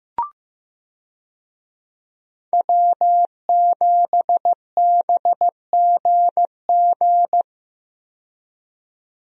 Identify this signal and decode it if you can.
Morse code, heard as a keyed tone, about 15 wpm, 700 Hz. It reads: W7BGG
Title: USA Call Signs - 5 Characters - 15wpm